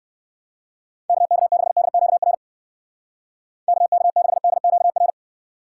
Morse code, heard as a keyed tone, about 59 wpm, 700 Hz. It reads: KK6DQR KK6DQR